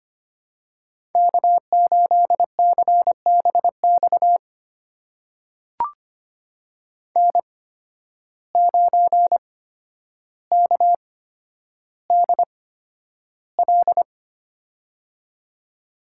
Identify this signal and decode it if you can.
Morse code, heard as a keyed tone, about 25 wpm, spaced as for 11 wpm, 700 Hz. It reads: K8CBX N9KDL
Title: USA Call Signs - 8x Character Spacing - 25wpm